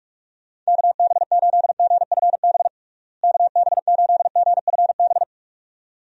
Morse code, heard as a keyed tone, about 44 wpm, 700 Hz. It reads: KB8GFB KB8GFB
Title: USA Call Signs - 6 Characters - Repeated 2x - 45wpm